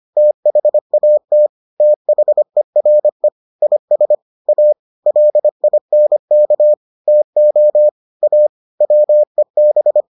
Morse code, heard as a keyed tone, about 25 wpm, 600 Hz. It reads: THAT THERE IS A LINK TO A WEB